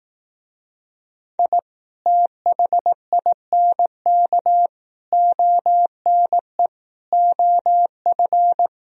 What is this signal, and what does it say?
Morse code, heard as a keyed tone, about 18 wpm, 700 Hz. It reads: I THINK ONE OF
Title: Speed-Racing - Sets of 5 Words 15wpm